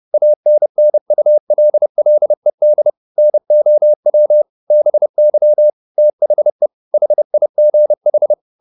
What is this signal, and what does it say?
Morse code, heard as a keyed tone, about 30 wpm, 600 Hz. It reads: ANNULLED NOW BY THE HIGH
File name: the old man and the sea / the old man and the sea 1 30wpm